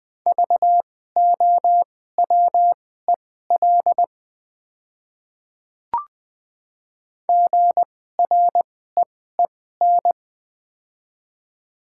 Morse code, heard as a keyed tone, about 20 wpm, spaced as for 14 wpm, 690 Hz. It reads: VOWEL GREEN
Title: 2x Character Spacing - Top 500 Words - 20wpm